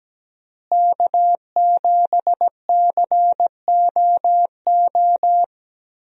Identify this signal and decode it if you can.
Morse code, heard as a keyed tone, about 17 wpm, 700 Hz. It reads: K7COO
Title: USA Call Signs - 5 Characters - Repeated 3x - 17wpm